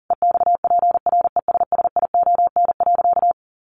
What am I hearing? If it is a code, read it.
Morse code, heard as a keyed tone, about 40 wpm, 700 Hz. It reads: EXPRESSION.
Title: Sentences from Top 1200 Words 40wpm